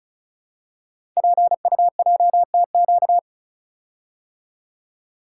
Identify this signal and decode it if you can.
Morse code, heard as a keyed tone, about 33 wpm, 690 Hz. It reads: PUJTQ
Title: Code Groups - 5 Letter:Numbers - 35wpm